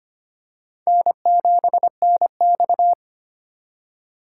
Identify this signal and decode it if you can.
Morse code, heard as a keyed tone, about 25 wpm, 700 Hz. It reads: N7NX
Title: USA Call Signs - 2x Character Spacing - 25wpm